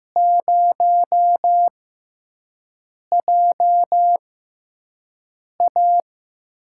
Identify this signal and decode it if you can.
Morse code, heard as a keyed tone, about 15 wpm, spaced as for 5 wpm, 700 Hz. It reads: 0JA
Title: USA Call Signs - 4 Characters - 6x Character Spacing - 15wpm